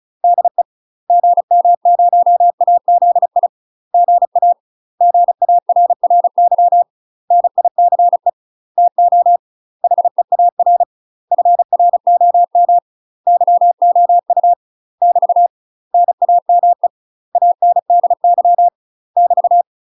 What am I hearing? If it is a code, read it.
Morse code, heard as a keyed tone, about 34 wpm, 700 Hz. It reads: DE GM0AZI GA GARRY NICE TO HEAR FROM YOU <BT> NAME ANDY <BT>
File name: UK MASTER QSOs TRAINING / 35WPM / UK MASTER QSOs 0024